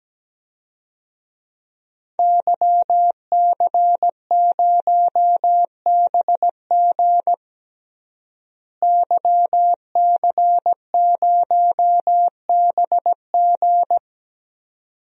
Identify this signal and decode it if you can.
Morse code, heard as a keyed tone, about 17 wpm, 700 Hz. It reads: YC0BG YC0BG
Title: DXCC-100 Call Signs - 5 Characters - Repeated 2x - 17wpm